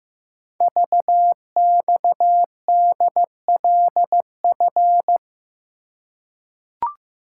Morse code, heard as a keyed tone, about 15 wpm, 700 Hz. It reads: VXDLF
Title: Code Groups - 5 Letters - 15wpm